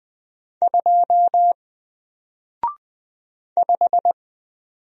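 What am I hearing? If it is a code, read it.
Morse code, heard as a keyed tone, about 20 wpm, 700 Hz. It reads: 2 5
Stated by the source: Single letter and number Morse drill at 20 wpm, from N4LCD.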